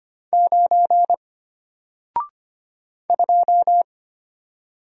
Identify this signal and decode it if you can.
Morse code, heard as a keyed tone, about 25 wpm, 700 Hz. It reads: 9 2
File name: Single-Letter-Number-25wpm